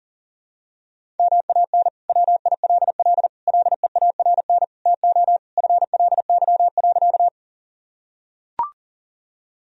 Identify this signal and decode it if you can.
Morse code, heard as a keyed tone, about 40 wpm, 700 Hz. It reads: MAN WILL LEARN TO FLY.
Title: Sentences from Top 1000 Words - Review Part 2 - 40wpm